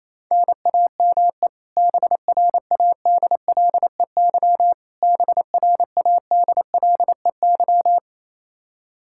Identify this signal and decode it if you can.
Morse code, heard as a keyed tone, about 28 wpm, 700 Hz. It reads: NAME BRADLEY BRADLEY
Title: Practice QSO - Exchange 1 - Beginning - 28wpm